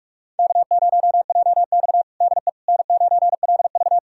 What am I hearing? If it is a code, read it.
Morse code, heard as a keyed tone, about 44 wpm, 690 Hz. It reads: K0JX DE N9LV
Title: USA Callsign DE Callsign - 4 Characters - 45wpm